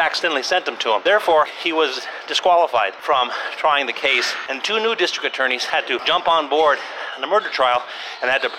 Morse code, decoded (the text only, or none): none